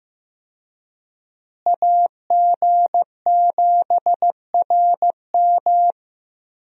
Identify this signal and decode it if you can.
Morse code, heard as a keyed tone, about 15 wpm, 700 Hz. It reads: AG7RM
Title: USA Call Signs - 5 Characters - Repeated 3x - 15wpm